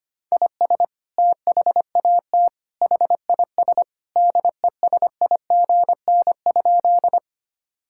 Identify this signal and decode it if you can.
Morse code, heard as a keyed tone, about 25 wpm, 700 Hz. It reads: IS THAT HIS DESIGN?
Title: Sentences from Top 1000 Words - Review Part 5 - 25wpm